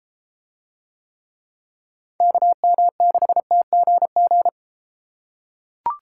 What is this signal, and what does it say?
Morse code, heard as a keyed tone, about 32 wpm, 700 Hz. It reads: KM6TGG